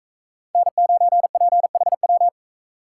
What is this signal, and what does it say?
Morse code, heard as a keyed tone, about 42 wpm, 700 Hz. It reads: N9PHW